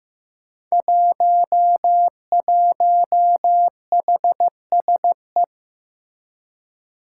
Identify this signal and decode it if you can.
Morse code, heard as a keyed tone, about 15 wpm, 700 Hz. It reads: 11HSE
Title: Five Challenging Letter-Number Combinations 15wpm